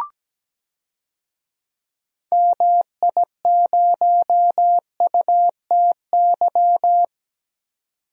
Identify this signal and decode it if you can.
Morse code, heard as a keyed tone, about 17 wpm, 700 Hz. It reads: MI0UTY
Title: DXCC-100 Call Signs - 6 Characters - 17wpm